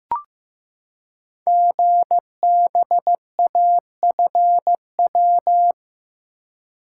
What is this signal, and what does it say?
Morse code, heard as a keyed tone, about 15 wpm, 700 Hz. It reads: GBAFW
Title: Code Groups - 5 Letters - 15wpm